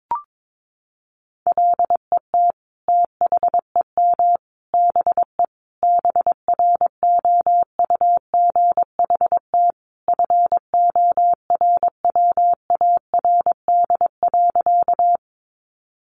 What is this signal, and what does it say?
Morse code, heard as a keyed tone, about 22 wpm, 700 Hz. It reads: LET THEM BE BROUGHT FORWARD.
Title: Sentences from Top 1300 Words 22wpm